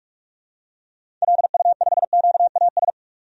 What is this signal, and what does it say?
Morse code, heard as a keyed tone, about 46 wpm, 700 Hz. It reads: LU5QAS
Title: DXCC-100 Call Signs - 6 Characters - 45wpm